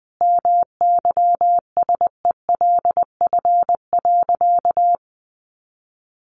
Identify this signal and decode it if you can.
Morse code, heard as a keyed tone, about 20 wpm, 700 Hz. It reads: MYSELF.
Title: Sentences from Top 1300 Words 20wpm